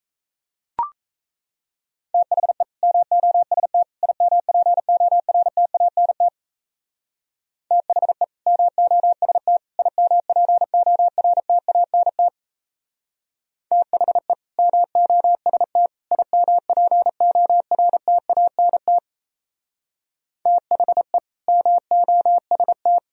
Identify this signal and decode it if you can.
Morse code, heard as a keyed tone, about 37 wpm, 700 Hz. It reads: THE MOST IMPORTANT THE MOST IMPORTANT THE MOST IMPORTANT THE MOST